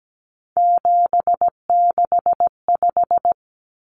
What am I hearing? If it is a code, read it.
Morse code, heard as a keyed tone, about 17 wpm, 700 Hz. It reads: 765
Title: Numbers - 3-Digit 17wpm